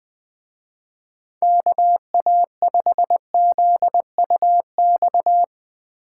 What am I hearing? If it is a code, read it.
Morse code, heard as a keyed tone, about 20 wpm, 700 Hz. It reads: KA5ZUX